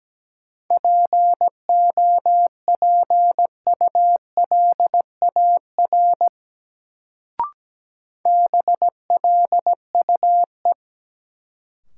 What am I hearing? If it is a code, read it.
Morse code, heard as a keyed tone, about 17 wpm, 700 Hz. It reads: POPULAR BLUE
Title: Sentences from Top 1200 Words - New Word Drill Alt - 17wpm